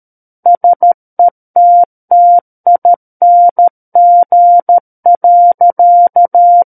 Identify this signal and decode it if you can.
Morse code, heard as a keyed tone, about 13 wpm, 700 Hz. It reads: SETTING.
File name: Book One / 04-Chapter 4 - 13 WPM